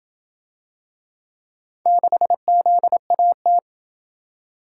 Morse code, heard as a keyed tone, about 27 wpm, 700 Hz. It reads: 6ZAT